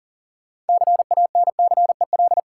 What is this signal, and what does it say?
Morse code, heard as a keyed tone, about 40 wpm, 700 Hz. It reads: CANCEL
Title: Sentences from Top 1200 Words - New Word Drill - 40wpm